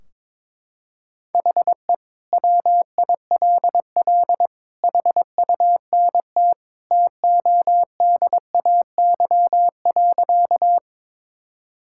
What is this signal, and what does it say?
Morse code, heard as a keyed tone, about 22 wpm, 700 Hz. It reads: HE WILL HUNT TODAY.
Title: Sentences from Top 1600 Words 22wpm